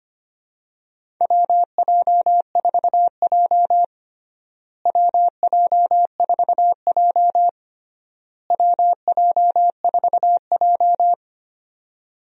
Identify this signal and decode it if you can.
Morse code, heard as a keyed tone, about 25 wpm, 700 Hz. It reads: WJ4J WJ4J WJ4J
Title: USA Call Signs - 4 Characters - Repeated 3x - 25wpm